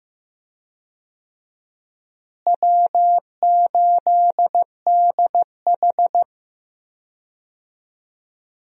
Morse code, heard as a keyed tone, about 15 wpm, 700 Hz. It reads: W8DH
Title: USA Call Signs - 4 Characters - Repeated 2x - 15wpm